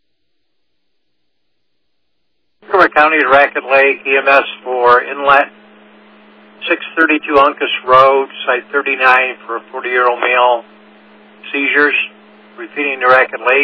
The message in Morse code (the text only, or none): none